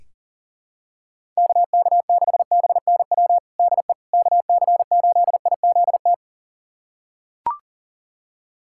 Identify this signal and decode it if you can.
Morse code, heard as a keyed tone, about 40 wpm, 700 Hz. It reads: KK6BNW DE KC8IZT
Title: USA Callsign DE Callsign 40wpm